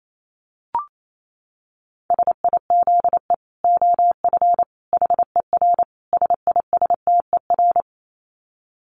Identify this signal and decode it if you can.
Morse code, heard as a keyed tone, about 28 wpm, 700 Hz. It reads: SIZE OF HER SISTER
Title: Dit Challenge 28wpm